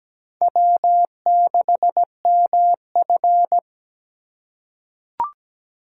Morse code, heard as a keyed tone, about 17 wpm, 700 Hz. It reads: W6MF
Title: USA Call Signs - 4 Characters - 2x Character Spacing - 17wpm